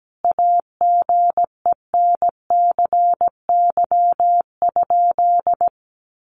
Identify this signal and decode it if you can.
Morse code, heard as a keyed tone, about 17 wpm, 700 Hz. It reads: AGENCY?
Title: Sentences from Top 1300 Words 17wpm